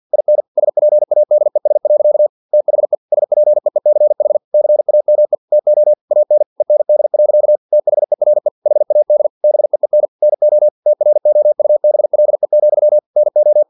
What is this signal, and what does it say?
Morse code, heard as a keyed tone, about 50 wpm, 600 Hz. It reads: IN SPADES. THE SPEECH CAME TO AN END. THERE HAD BEEN NO TROUBLE, NO
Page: Never 49wpm